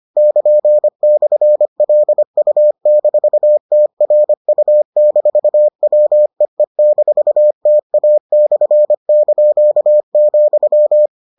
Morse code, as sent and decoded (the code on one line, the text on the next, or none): -.--. -..-. .-.. ..- -....- - .-. ..- -....- .-- . -....- - .- -..-. -.--.- --..--
<KN>/LU-TRU-WEE-TA/),